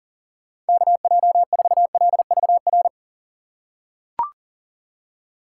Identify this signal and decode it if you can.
Morse code, heard as a keyed tone, about 40 wpm, 700 Hz. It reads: KJ4LVR
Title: USA Call Signs - 6 Characters - 6x Character Spacing - 40wpm